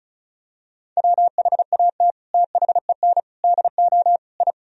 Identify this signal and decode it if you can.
Morse code, heard as a keyed tone, about 36 wpm, 700 Hz. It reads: WHAT THEN DO I
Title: Sentences from Top 1000 Words - Review Part 2 - 35wpm